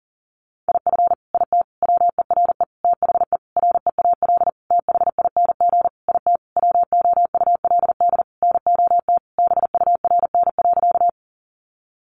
Morse code, heard as a keyed tone, about 40 wpm, 700 Hz. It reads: IF IT WERE THE REAL THING IT WOULD NOT BURN.
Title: Sentences from Top 1300 Words 40wpm